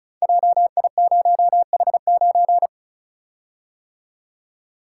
Morse code, heard as a keyed tone, about 36 wpm, 700 Hz. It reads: JI0H9
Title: Code Groups - 5 Letter:Numbers - 35wpm